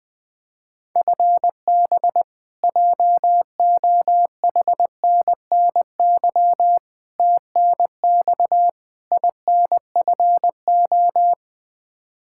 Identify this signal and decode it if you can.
Morse code, heard as a keyed tone, about 21 wpm, 700 Hz. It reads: FB JOHNNY TNX INFO
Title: Practice QSO - Exchange 3 - More Info - 20wpm